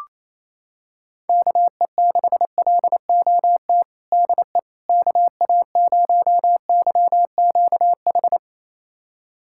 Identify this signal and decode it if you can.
Morse code, heard as a keyed tone, about 28 wpm, 700 Hz. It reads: KE6LOT DE KA0YQH
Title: USA Callsign DE Callsign 28wpm